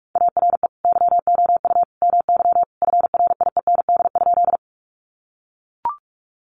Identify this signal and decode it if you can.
Morse code, heard as a keyed tone, about 44 wpm, 700 Hz. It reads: ARE YOU MY FRIEND?
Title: Sentences from Top 300 Words 45wpm